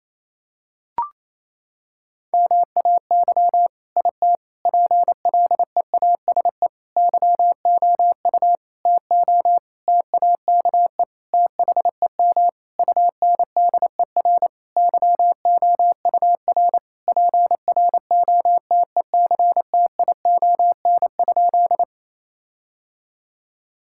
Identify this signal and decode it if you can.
Morse code, heard as a keyed tone, about 28 wpm, 700 Hz. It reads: MAY IT PLEASE YOU TO TAKE THEM UNDER YOUR PROTECTION?
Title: Sentences from Top 1600 Words 28wpm